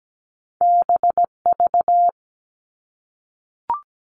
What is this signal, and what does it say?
Morse code, heard as a keyed tone, about 17 wpm, 700 Hz. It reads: BV
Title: All Cut Numbers - 2-Digit 17wpm